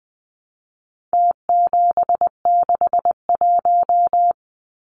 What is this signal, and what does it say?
Morse code, heard as a keyed tone, about 19 wpm, 700 Hz. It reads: T761
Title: Numbers - Contest Style - 4-Digit 20wpm